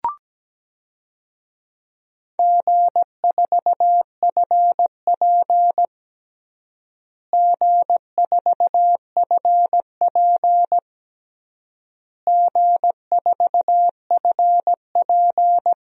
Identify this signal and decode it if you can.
Morse code, heard as a keyed tone, about 17 wpm, 700 Hz. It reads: G4FP G4FP G4FP
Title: DXCC-100 Call Signs - 4 Characters - Repeated 3x - 17wpm